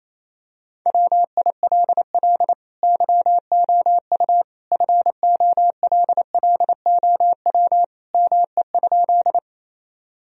Morse code, heard as a keyed tone, about 28 wpm, 700 Hz. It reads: WILL YOU FOLLOW ME?
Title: Sentences from Top 1000 Words - Review Part 1 - 28wpm